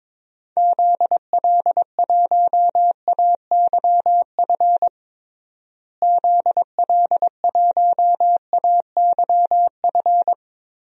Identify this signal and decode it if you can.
Morse code, heard as a keyed tone, about 22 wpm, 700 Hz. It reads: ZL1AYF ZL1AYF
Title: DXCC-100 Call Signs - 6 Characters - Repeated 3x - 22wpm